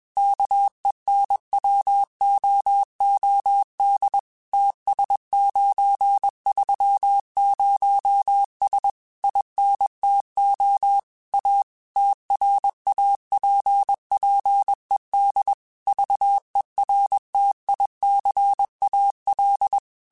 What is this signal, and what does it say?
Morse code, heard as a keyed tone, about 21 wpm, 800 Hz. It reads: KENWOOD TS930S INTO A TRAPPED VERTICAL